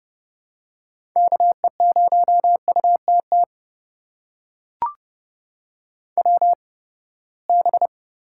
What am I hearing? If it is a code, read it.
Morse code, heard as a keyed tone, about 31 wpm, spaced as for 21 wpm, 700 Hz. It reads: KE0UTT WB
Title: USA Call Signs - 6 Characters - 8x Character Spacing - 30wpm